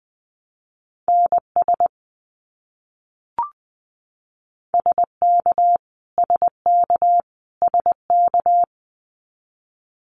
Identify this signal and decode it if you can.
Morse code, heard as a keyed tone, about 20 wpm, 700 Hz. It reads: NS SK SK SK
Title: Canadian Provinces and Territories - Repeated 3x - 20wpm